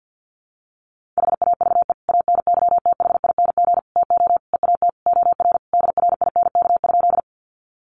Morse code, heard as a keyed tone, about 50 wpm, 700 Hz. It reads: HAVE ANYTHING TO EAT OR DRINK?
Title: Sentences from Top 1500 Words 50wpm